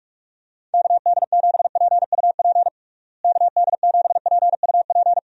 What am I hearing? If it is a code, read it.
Morse code, heard as a keyed tone, about 41 wpm, 700 Hz. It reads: KD7PUP KD7PUP
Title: USA Call Signs - 6 Characters - Repeated 2x - 45wpm